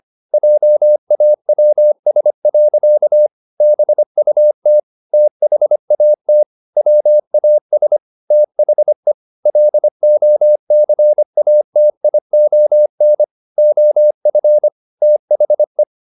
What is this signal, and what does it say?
Morse code, heard as a keyed tone, about 25 wpm, 600 Hz. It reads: JAWS. BUT THAT WAS THE LOCATION OF THE